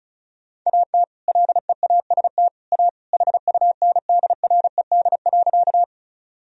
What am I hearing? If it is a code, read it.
Morse code, heard as a keyed tone, about 34 wpm, 700 Hz. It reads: AT LEAST A HUNDRED.
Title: Sentences from Top 1000 Words - Review Part 2 - 35wpm